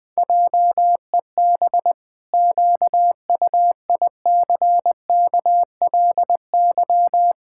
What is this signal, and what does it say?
Morse code, heard as a keyed tone, about 20 wpm, 700 Hz. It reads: JEB QUICKLY
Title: Pangram Challenge v2 - 20wpm